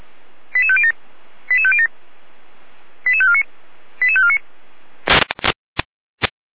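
Five-tone selective call (ZVEI), heard as quick runs of five tones, tones of about 70 ms. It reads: 88507 88507 88549 88549
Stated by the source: ZVEI five-tone selective call received on SDR.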